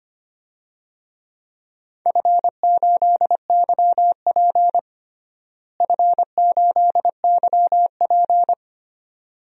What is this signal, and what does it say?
Morse code, heard as a keyed tone, about 25 wpm, 700 Hz. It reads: F8YP F8YP
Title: DXCC-100 Call Signs - 4 Characters - Repeated 2x - 25wpm